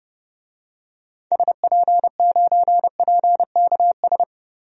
Morse code, heard as a keyed tone, about 30 wpm, 700 Hz. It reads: SP9PKS
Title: DXCC-100 Call Signs - 6 Characters - Repeated 2x - 30wpm